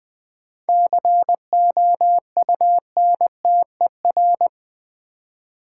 Morse code, heard as a keyed tone, about 21 wpm, 700 Hz. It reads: COUNTER
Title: Seven Letter Words - 20wpm